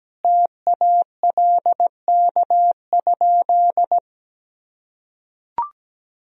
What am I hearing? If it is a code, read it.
Morse code, heard as a keyed tone, about 17 wpm, 700 Hz. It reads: TALK?
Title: Sentences from Top 1000 Words - Review Part 1 - 17wpm